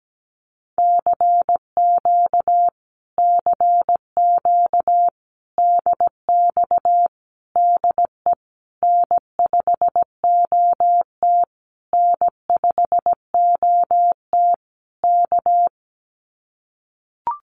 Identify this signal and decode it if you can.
Morse code, heard as a keyed tone, about 17 wpm, 700 Hz. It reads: CQ CQ DX DE N5OT N5OT K
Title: Common QSO Phrases Encore 17wpm